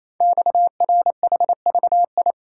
Morse code, heard as a keyed tone, about 27 wpm, 700 Hz. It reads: XRHVI
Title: Code Groups - 5 Letters - 28wpm